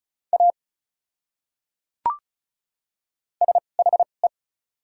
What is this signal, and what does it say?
Morse code, heard as a keyed tone, about 34 wpm, spaced as for 5 wpm, 700 Hz. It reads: AS HE
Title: 2x Character Spacing - Top 100 Words - 35wpm